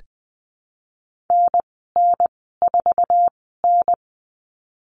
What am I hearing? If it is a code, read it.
Morse code, heard as a keyed tone, about 20 wpm, spaced as for 15 wpm, 700 Hz. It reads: NN4N
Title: 048 Lesson 9g - Number 4 - Call Signs 20wpm